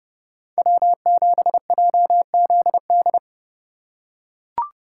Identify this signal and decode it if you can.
Morse code, heard as a keyed tone, about 30 wpm, 700 Hz. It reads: W7JZD
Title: USA Call Signs - 5 Characters - 3x Character Spacing - 30wpm